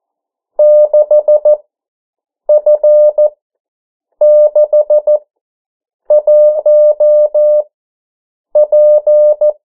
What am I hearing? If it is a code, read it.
Morse code, heard as a keyed tone, about 14 wpm, spaced as for 7 wpm, 600 Hz. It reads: 6F61P
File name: morse mp3 / 09 urok 8